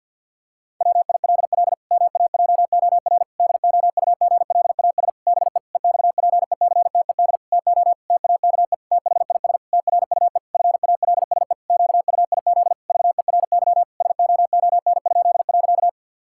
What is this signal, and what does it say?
Morse code, heard as a keyed tone, about 50 wpm, 700 Hz. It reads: WILL MAJOR DOUGLAS BE EXPECTED TO TAKE THIS TRUE FALSE QUIZ VERY SOON?.
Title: Pangram Challenge v2 - 50wpm